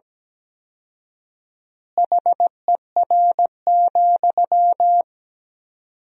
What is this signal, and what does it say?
Morse code, heard as a keyed tone, about 17 wpm, 700 Hz. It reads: HER,
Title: 4x Word-Spacing - Sentences from Top 200 Words - 17wpm